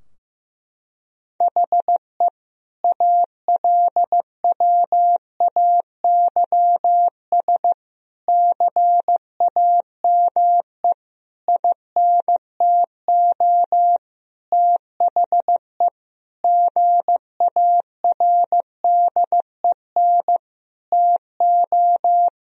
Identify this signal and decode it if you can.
Morse code, heard as a keyed tone, about 15 wpm, 700 Hz. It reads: HE ALWAYS CAME INTO THE GARDEN TO
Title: Sentences from Top 1000 Words - Review Part 4 - 15wpm